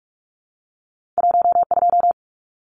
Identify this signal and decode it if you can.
Morse code, heard as a keyed tone, about 46 wpm, 700 Hz. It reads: 12
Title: Numbers - 2-Digit 45wpm